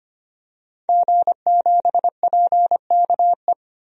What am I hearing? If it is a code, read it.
Morse code, heard as a keyed tone, about 25 wpm, 700 Hz. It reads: G7PKE
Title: DXCC-100 Call Signs - 5 Characters - Repeated 3x - 25wpm